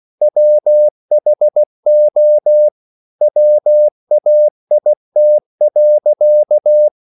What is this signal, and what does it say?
Morse code, heard as a keyed tone, about 16 wpm, 600 Hz. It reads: WHO WAIT.
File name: CW/Practice